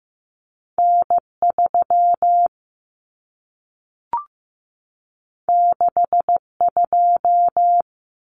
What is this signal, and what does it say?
Morse code, heard as a keyed tone, about 15 wpm, 700 Hz. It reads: N 3 6 2
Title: Common Cut Numbers - 2-Digit 15wpm